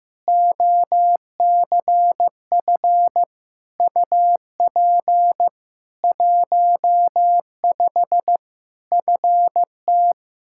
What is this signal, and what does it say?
Morse code, heard as a keyed tone, about 15 wpm, 700 Hz. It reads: OCF UP 15 FT
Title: Practice QSO - Exchange 2 - Info - 15wpm